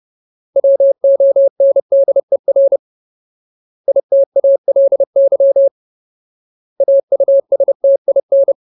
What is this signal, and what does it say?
Morse code, heard as a keyed tone, about 30 wpm, 550 Hz. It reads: WONDER ITALY AUSTIN
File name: Ultimate Word List / 30WPM Ultimate